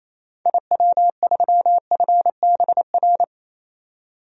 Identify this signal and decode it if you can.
Morse code, heard as a keyed tone, about 28 wpm, 700 Hz. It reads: IW3FBR
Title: DXCC-100 Call Signs - 6 Characters - 28wpm